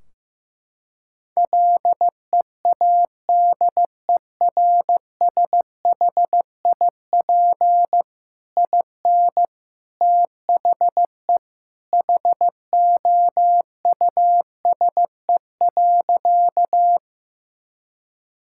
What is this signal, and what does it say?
Morse code, heard as a keyed tone, about 15 wpm, 700 Hz. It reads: LEADERSHIP IN THE HOUSE.
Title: Sentences from Top 1600 Words 15wpm